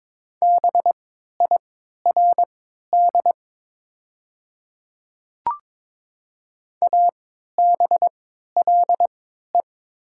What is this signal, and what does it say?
Morse code, heard as a keyed tone, about 22 wpm, spaced as for 13 wpm, 700 Hz. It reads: BIRD ABLE
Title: 3x Character Spacing - Top 500 Words - 22wpm